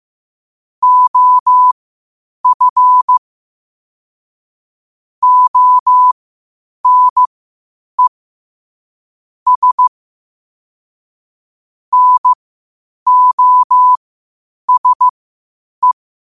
Morse code, heard as a keyed tone, about 15 wpm, spaced as for 8 wpm, 1000 Hz. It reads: OF ONE S NOSE
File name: qotdfiles / 8wpm